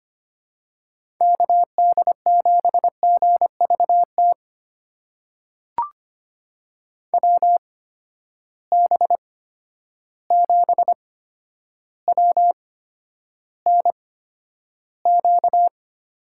Morse code, heard as a keyed tone, about 25 wpm, spaced as for 11 wpm, 700 Hz. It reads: KD7GVT WB7WNQ
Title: USA Call Signs - 6 Characters - 8x Character Spacing - 25wpm